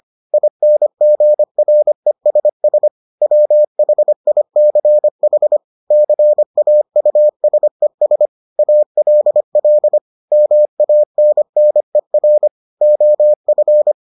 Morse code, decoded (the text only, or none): INGRESS WHICH CAUSES ALL MANNER OF